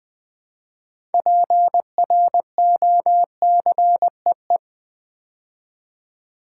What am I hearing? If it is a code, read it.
Morse code, heard as a keyed tone, about 20 wpm, 700 Hz. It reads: PROCEE
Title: Build A Word - 3 Letters At A Time - 8-Letter Words - 20wpm